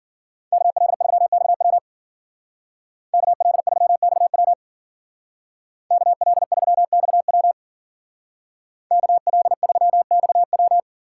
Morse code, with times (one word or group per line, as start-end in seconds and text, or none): 0.5-1.8 KL3XW
3.1-4.5 KL3XW
5.9-7.5 KL3XW
8.9-10.8 KL3XW